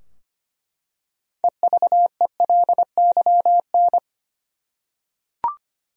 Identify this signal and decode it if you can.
Morse code, heard as a keyed tone, about 25 wpm, 700 Hz. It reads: EVELYN
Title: USA Common Names - 25wpm